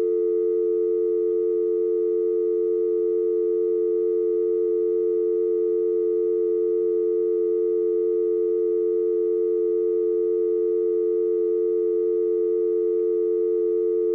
Dial tone (North America), 350 + 440 Hz.
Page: telephone-electronic-dial-tone-internal-office